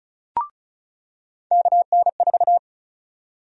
Morse code, heard as a keyed tone, about 36 wpm, 700 Hz. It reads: KN4
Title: USA Call Sign Prefixes - 35wpm